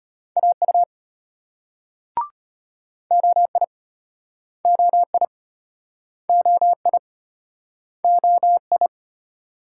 Morse code, heard as a keyed tone, about 35 wpm, 700 Hz. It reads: AU OI OI OI OI